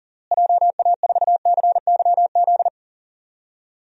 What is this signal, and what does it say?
Morse code, heard as a keyed tone, about 40 wpm, 700 Hz. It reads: JA4CYZ
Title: DXCC-100 Call Signs - 6 Characters - Repeated 3x - 40wpm